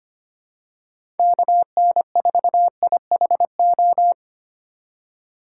Morse code, heard as a keyed tone, about 25 wpm, 700 Hz. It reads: KN4IHO
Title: USA Call Signs - 6 Characters - 8x Character Spacing - 25wpm